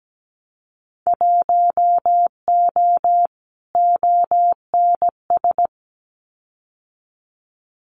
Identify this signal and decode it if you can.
Morse code, heard as a keyed tone, about 17 wpm, 700 Hz. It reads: 1O ONS
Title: Winter Field Day Exchanges 17wpm